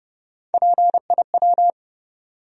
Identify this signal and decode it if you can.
Morse code, heard as a keyed tone, about 29 wpm, 700 Hz. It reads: PIW
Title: USA Call Sign Postfixes - 3 Letters 30wpm